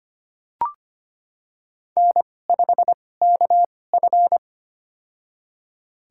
Morse code, heard as a keyed tone, about 25 wpm, spaced as for 18 wpm, 700 Hz. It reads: N5KF
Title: USA Call Signs - 4 Characters - 2x Character Spacing - 25wpm